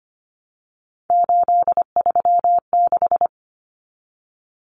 Morse code, heard as a keyed tone, about 25 wpm, 700 Hz. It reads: 836
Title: Common Cut Numbers - 3-Digit 25wpm